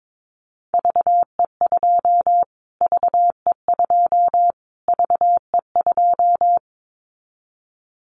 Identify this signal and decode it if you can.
Morse code, heard as a keyed tone, about 22 wpm, 700 Hz. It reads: VE2 VE2 VE2
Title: Callsign Prefixes for Top 25 DXCC - Repeated 3x - 22wpm